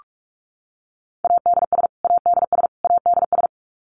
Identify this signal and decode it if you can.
Morse code, heard as a keyed tone, about 44 wpm, 700 Hz. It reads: ADS ADS ADS
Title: Abbreviations Repeated 3x - 45wpm